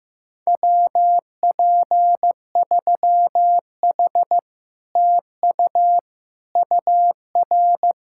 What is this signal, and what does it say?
Morse code, heard as a keyed tone, about 15 wpm, 700 Hz. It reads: WP3H TU UR
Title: Speed-Racing Common QSO Phrases Encore - 15wpm